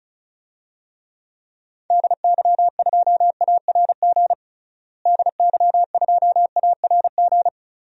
Morse code, heard as a keyed tone, about 34 wpm, 700 Hz. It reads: DY2ARG DY2ARG
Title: DXCC-100 Call Signs - 6 Characters - Repeated 3x - 35wpm